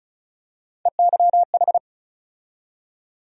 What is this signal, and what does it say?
Morse code, heard as a keyed tone, about 36 wpm, 700 Hz. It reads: EYH
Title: USA Call Sign Postfixes - 3 Letters 35wpm